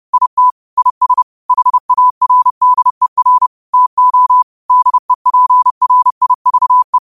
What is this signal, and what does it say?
Morse code, heard as a keyed tone, about 31 wpm, 1000 Hz. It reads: IT IS HARDER TO DEPRIVE
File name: qotdfiles / 30wpm